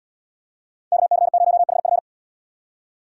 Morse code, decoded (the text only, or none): KC0HF